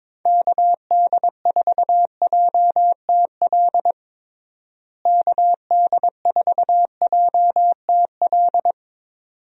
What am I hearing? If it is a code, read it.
Morse code, heard as a keyed tone, about 21 wpm, 700 Hz. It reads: KD4JTL KD4JTL
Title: USA Call Signs - 6 Characters - Repeated 2x - 22wpm